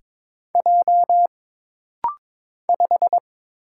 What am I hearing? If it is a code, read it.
Morse code, heard as a keyed tone, about 22 wpm, 700 Hz. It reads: J 5
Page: This rapid-fire timing is geared to push your ICR (Instant Character Recognition) skills.